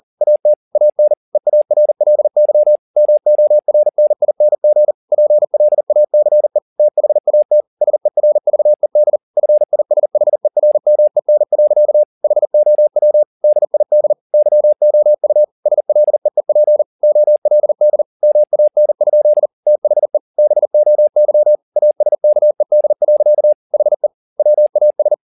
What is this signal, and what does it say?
Morse code, heard as a keyed tone, about 40 wpm, 600 Hz. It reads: AT AN EARLY MORNING PLACE THAT SERVED FISHERMEN. HOW DID YOU SLEEP OLD MAN? THE BOY ASKED. HE WAS